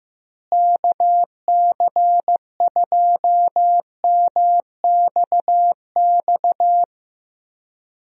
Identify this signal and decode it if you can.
Morse code, heard as a keyed tone, about 15 wpm, 700 Hz. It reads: KC2MXX
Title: USA Call Signs - 6 Characters - Repeated 3x - 15wpm